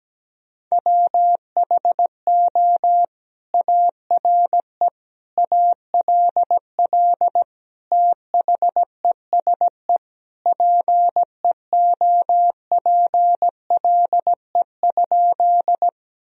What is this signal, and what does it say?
Morse code, heard as a keyed tone, about 17 wpm, 700 Hz. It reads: WHO ARE ALL THESE PEOPLE?
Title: Sentences from Top 1000 Words - Review Part 1 - 17wpm